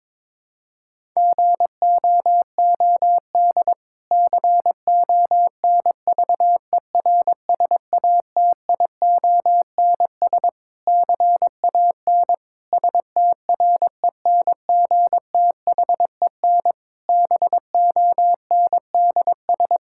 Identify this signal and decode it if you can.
Morse code, heard as a keyed tone, about 21 wpm, 700 Hz. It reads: GOOD CONVERSATIONS CAN STRENGTHEN BONDS
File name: Difficult Sentences with Hints - 22wpm